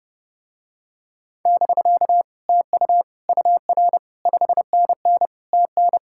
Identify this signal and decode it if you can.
Morse code, heard as a keyed tone, about 30 wpm, 700 Hz. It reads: <BK> TU UR 5NN TN